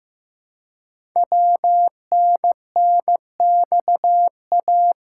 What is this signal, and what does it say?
Morse code, heard as a keyed tone, about 15 wpm, 700 Hz. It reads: WNNXA
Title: Code Groups - 5 Letters - 15wpm